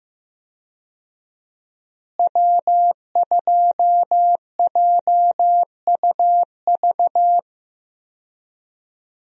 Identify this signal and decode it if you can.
Morse code, heard as a keyed tone, about 16 wpm, 700 Hz. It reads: W2JUV
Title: USA Call Signs - 5 Characters - Repeated 2x - 15wpm